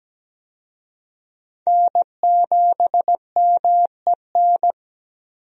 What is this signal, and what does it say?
Morse code, heard as a keyed tone, about 17 wpm, 700 Hz. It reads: N7MEN
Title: USA Call Signs - 5 Characters - 17wpm